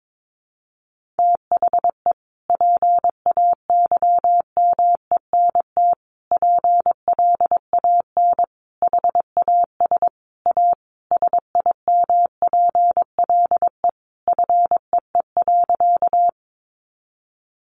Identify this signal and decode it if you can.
Morse code, heard as a keyed tone, about 22 wpm, 700 Hz. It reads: THE PAYMENT PLAN HAS A SIMPLE FEE.
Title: Sentences from Top 1500 Words 22wpm